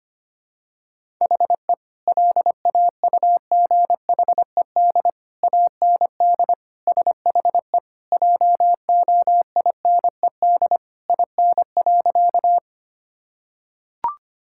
Morse code, heard as a keyed tone, about 25 wpm, 700 Hz. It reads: HE LAUGHED AND SHE JOINED IN.
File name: practice / 25wpm / Sentences with 2-Word Phrases - 25wpm